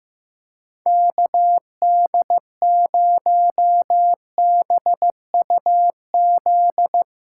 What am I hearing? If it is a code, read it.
Morse code, heard as a keyed tone, about 15 wpm, 700 Hz. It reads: KD0BUZ
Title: USA Callsign DE Callsign - 6 Characters - 15wpm